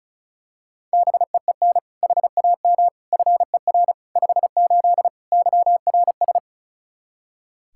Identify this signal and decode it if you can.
Morse code, heard as a keyed tone, about 36 wpm, 700 Hz. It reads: BEEN HAM FER 58 YRS
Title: Practice QSO - Exchange 3 - More Info - 35wpm